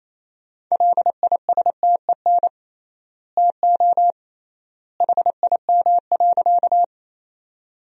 Morse code, heard as a keyed tone, about 28 wpm, 700 Hz. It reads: LISTEN TO HIM.
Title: 3x Word-Spacing - Sentences from Top 400 Words - 28wpm